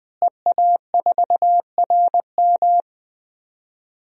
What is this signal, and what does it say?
Morse code, heard as a keyed tone, about 20 wpm, 700 Hz. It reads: EA4RM
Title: DXCC-100 Call Signs - 5 Characters - Repeated 3x - 20wpm